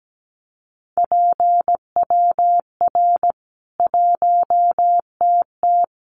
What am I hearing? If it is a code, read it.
Morse code, heard as a keyed tone, about 17 wpm, 700 Hz. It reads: PWR 1TT
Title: Common QSO Phrases 17wpm